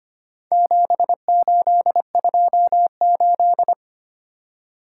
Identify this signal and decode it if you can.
Morse code, heard as a keyed tone, about 25 wpm, 700 Hz. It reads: 7828
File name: practice / 25wpm / Four Letter-Number Combinations - 25wpm